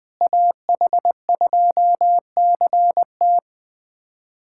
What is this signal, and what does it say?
Morse code, heard as a keyed tone, about 21 wpm, 700 Hz. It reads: AH2CT
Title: USA Callsign DE Callsign - 5 Characters - 20wpm